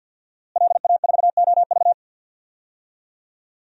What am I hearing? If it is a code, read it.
Morse code, heard as a keyed tone, about 51 wpm, 700 Hz. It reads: LA4OV
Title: DXCC-100 Call Signs - 5 Characters - Repeated 3x - 50wpm